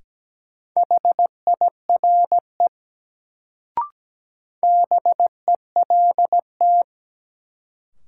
Morse code, heard as a keyed tone, about 17 wpm, 700 Hz. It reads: HIRE BELT
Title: Sentences from Top 1400 Words - New Word Drill - 17wpm